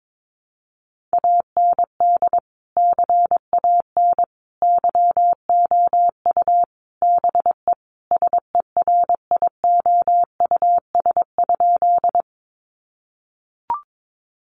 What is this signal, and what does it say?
Morse code, heard as a keyed tone, about 22 wpm, 700 Hz. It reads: AND CAN YOU BE SERIOUS?
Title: Sentences from Top 1100 Words 22wpm